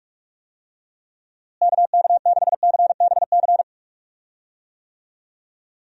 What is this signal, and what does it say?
Morse code, heard as a keyed tone, about 41 wpm, 700 Hz. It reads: KK6CBC
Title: USA Call Signs - 6 Characters - Repeated 3x - 45wpm